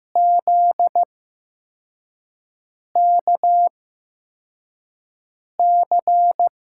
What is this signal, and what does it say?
Morse code, heard as a keyed tone, about 15 wpm, spaced as for 4 wpm, 700 Hz. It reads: ZKC
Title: USA Call Signs - 6 Characters - 8x Character Spacing - 15wpm